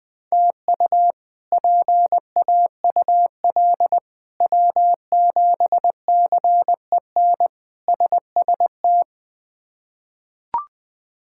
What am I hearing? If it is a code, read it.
Morse code, heard as a keyed tone, about 20 wpm, 700 Hz. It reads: TU PAUL W7CEN SST
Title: K1USN SST QSO Practice - 20wpm